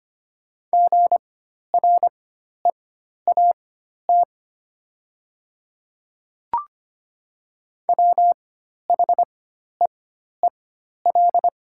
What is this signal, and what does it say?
Morse code, heard as a keyed tone, about 25 wpm, spaced as for 12 wpm, 690 Hz. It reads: GREAT WHEEL
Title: 4x Character Spacing - Top 500 Words - 25wpm